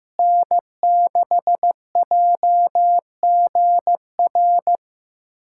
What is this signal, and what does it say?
Morse code, heard as a keyed tone, about 15 wpm, 700 Hz. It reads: N6JGR
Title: USA Callsign DE Callsign - 5 Characters - 15wpm